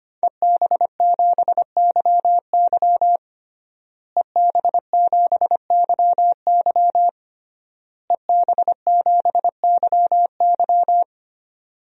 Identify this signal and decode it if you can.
Morse code, heard as a keyed tone, about 25 wpm, 700 Hz. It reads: EB7YY EB7YY EB7YY
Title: DXCC-100 Call Signs - 5 Characters - Repeated 3x - 25wpm